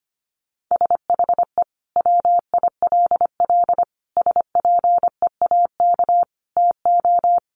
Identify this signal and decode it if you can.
Morse code, heard as a keyed tone, about 25 wpm, 700 Hz. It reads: SHE WILL SPEAK TO
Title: Sentences from Top 1100 Words 25wpm